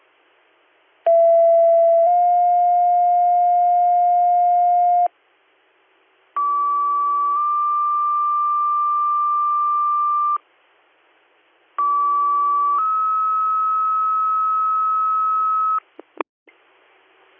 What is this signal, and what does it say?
Two-tone page, heard as a one-second tone, then a three-second tone, about 670 then 710 Hz, about 1150 then 1180 Hz, about 1150 then 1290 Hz. It is heard from 1.1 to 15.8 s.